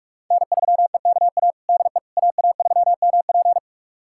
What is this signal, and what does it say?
Morse code, heard as a keyed tone, about 41 wpm, 700 Hz. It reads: N3EKA DE AA3MP
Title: USA Callsign DE Callsign - 5 Characters - 45wpm